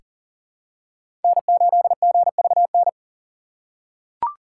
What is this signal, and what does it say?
Morse code, heard as a keyed tone, about 39 wpm, 700 Hz. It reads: N8GVN
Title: USA Call Signs - 5 Characters - 4x Character Spacing - 40wpm